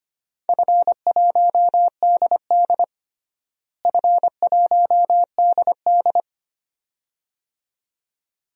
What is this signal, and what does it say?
Morse code, heard as a keyed tone, about 25 wpm, 700 Hz. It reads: F1DD F1DD
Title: DXCC-100 Call Signs - 4 Characters - Repeated 2x - 25wpm